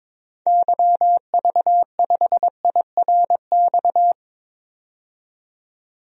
Morse code, heard as a keyed tone, about 22 wpm, 700 Hz. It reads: YV5IRX
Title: DXCC-100 Call Signs - 6 Characters - 22wpm